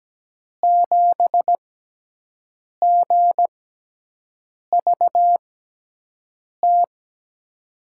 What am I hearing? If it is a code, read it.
Morse code, heard as a keyed tone, about 17 wpm, spaced as for 6 wpm, 700 Hz. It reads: 7GVT
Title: USA Call Signs - 6x Character Spacing - 17wpm